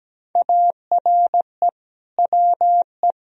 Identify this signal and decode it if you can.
Morse code, heard as a keyed tone, about 17 wpm, 700 Hz. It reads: ARE WE
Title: Sentences from Top 1000 Words - Review Part 4 - 17wpm